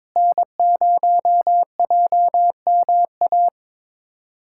Morse code, heard as a keyed tone, about 21 wpm, 700 Hz. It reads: N0JMA
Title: USA Callsign DE Callsign 22wpm